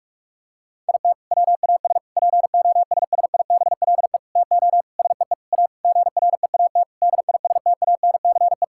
Morse code, heard as a keyed tone, about 46 wpm, 700 Hz. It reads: IT WAS POSSIBLE TO SEE A GREAT DISTANCE
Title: Sentences from Top 1600 Words 45wpm